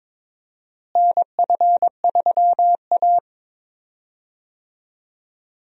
Morse code, heard as a keyed tone, about 21 wpm, 700 Hz. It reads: NF3A
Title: USA Call Signs - 4 Characters - Repeated 2x - 22wpm